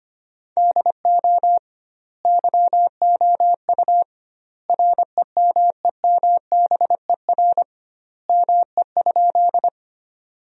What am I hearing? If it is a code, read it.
Morse code, heard as a keyed tone, about 25 wpm, 700 Hz. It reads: DO YOU REMEMBER ME?
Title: 2x Word-Spacing - Sentences from Top 400 Words - 25wpm